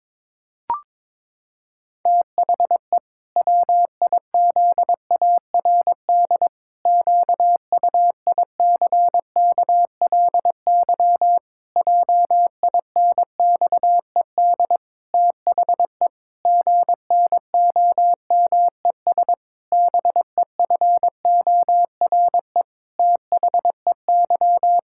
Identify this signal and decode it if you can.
Morse code, heard as a keyed tone, about 22 wpm, 700 Hz. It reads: THE WIZARD QUICKLY JINXED THE GNOMES BEFORE THEY
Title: Pangram Challenge v2 - 22wpm